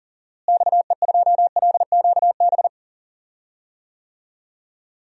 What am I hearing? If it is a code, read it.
Morse code, heard as a keyed tone, about 40 wpm, 700 Hz. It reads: XE2LQB
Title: DXCC-100 Call Signs - 6 Characters - Repeated 3x - 40wpm